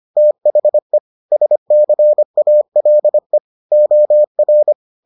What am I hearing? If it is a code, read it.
Morse code, heard as a keyed tone, about 25 wpm, 600 Hz. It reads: THE SCALE OR